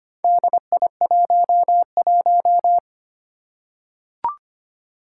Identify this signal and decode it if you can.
Morse code, heard as a keyed tone, about 25 wpm, 700 Hz. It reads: DI11
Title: Four Challenging Letter-Number Combinations 25wpm